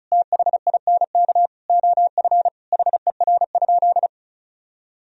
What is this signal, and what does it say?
Morse code, heard as a keyed tone, about 34 wpm, 700 Hz. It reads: THINK OF HER?
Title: Sentences from Top 1000 Words - Review Part 4 - 35wpm